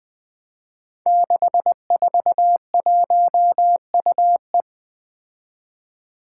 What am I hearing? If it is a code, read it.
Morse code, heard as a keyed tone, about 20 wpm, 700 Hz. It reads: 641UE
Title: Five Challenging Letter-Number Combinations 20wpm